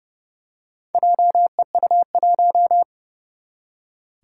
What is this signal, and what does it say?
Morse code, heard as a keyed tone, about 31 wpm, 700 Hz. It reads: JEU1
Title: Four Challenging Letter-Number Combinations 30wpm